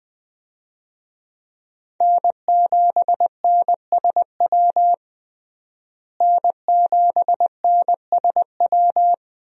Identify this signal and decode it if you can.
Morse code, heard as a keyed tone, about 20 wpm, 700 Hz. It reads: N7NSW N7NSW
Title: USA Call Signs - 5 Characters - Repeated 2x - 20wpm